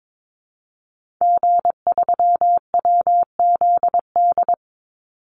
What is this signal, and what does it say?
Morse code, heard as a keyed tone, about 22 wpm, 700 Hz. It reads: G3WZD
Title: Call Signs Repeated 2x - 22wpm